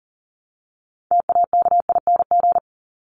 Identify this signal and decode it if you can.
Morse code, heard as a keyed tone, about 40 wpm, 700 Hz. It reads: TAKING
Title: ING Suffix - Top 100 Words with Base Words 40wpm